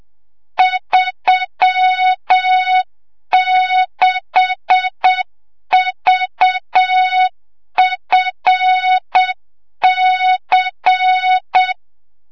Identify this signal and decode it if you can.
Morse code, heard as a keyed tone, about 7 wpm, 3700 Hz, with noise. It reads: 36VFC
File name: Morze-HA2MN / hang / 25 GYAKORLAT